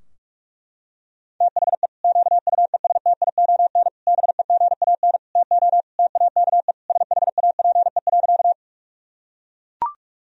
Morse code, heard as a keyed tone, about 42 wpm, 700 Hz. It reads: THE QUESTION BEGAN TO TAKE SHAPE.
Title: Sentences from Top 1000 Words - Review Part 3 - 45wpm